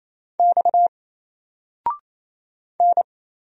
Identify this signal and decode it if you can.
Morse code, heard as a keyed tone, about 28 wpm, 700 Hz. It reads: X N
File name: Single-Letter-Number-28wpm